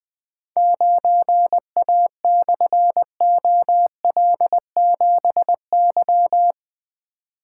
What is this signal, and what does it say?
Morse code, heard as a keyed tone, about 20 wpm, 700 Hz. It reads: 9A/OL7Y
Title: Difficult Call Signs 20wpm